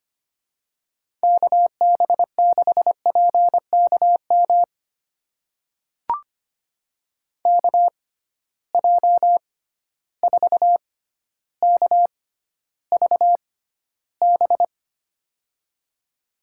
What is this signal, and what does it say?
Morse code, heard as a keyed tone, about 25 wpm, spaced as for 13 wpm, 700 Hz. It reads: KB6PKM KJ4KVB
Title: USA Call Signs - 6 Characters - 6x Character Spacing - 25wpm